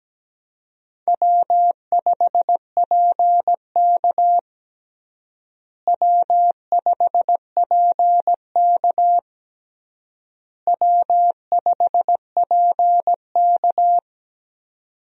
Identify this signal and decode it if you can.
Morse code, heard as a keyed tone, about 17 wpm, 700 Hz. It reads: W5PK W5PK W5PK
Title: USA Call Signs - 4 Characters - Repeated 3x - 17wpm